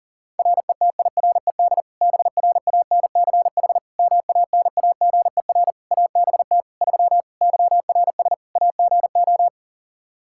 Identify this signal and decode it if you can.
Morse code, heard as a keyed tone, about 40 wpm, 700 Hz. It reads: RETIRED BRANCH MANAGER ABT 3 YRS AGO
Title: Practice QSO - Exchange 3 - More Info - 40wpm